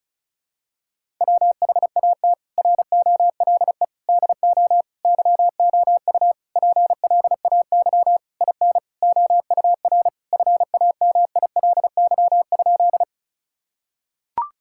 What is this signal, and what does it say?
Morse code, heard as a keyed tone, about 34 wpm, 700 Hz. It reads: WHAT ROLE DO YOU PLAY IN OUR FAMILY?
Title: Sentences from Top 1000 Words - Review Part 6 - 35wpm